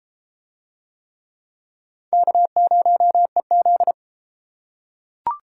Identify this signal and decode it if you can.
Morse code, heard as a keyed tone, about 32 wpm, 700 Hz. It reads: K0EZ